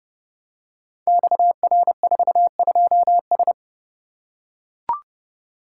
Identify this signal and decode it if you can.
Morse code, heard as a keyed tone, about 30 wpm, 700 Hz. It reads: XR42S
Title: Code Groups - 5 Letter:Numbers - 30wpm